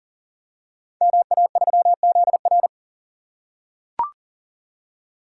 Morse code, heard as a keyed tone, about 39 wpm, 700 Hz. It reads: MA3ZR
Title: Code Groups - 5 Letter:Numbers - 40wpm